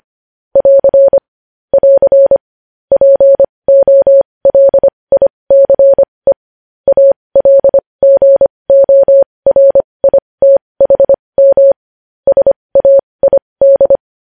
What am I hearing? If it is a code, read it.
Morse code, heard as a keyed tone, about 25 wpm, spaced as for 20 wpm, 550 Hz. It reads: <AR> <AR> POLICE ALGORITHM SAID
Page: news-headlines-in-morse-code-at-20-wpm